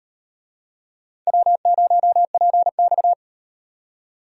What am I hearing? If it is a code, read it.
Morse code, heard as a keyed tone, about 38 wpm, 700 Hz. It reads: W0PX